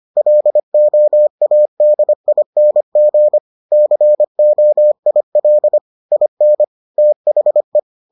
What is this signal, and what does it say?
Morse code, heard as a keyed tone, about 25 wpm, 600 Hz. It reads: LOADING COIL IN THE